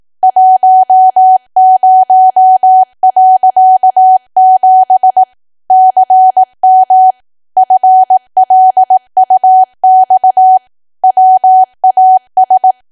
Morse code, heard as a keyed tone, about 18 wpm, 750 Hz. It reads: 10.7 CM FLUX WAS
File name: cwa / W1AW-Level2 / 18WPM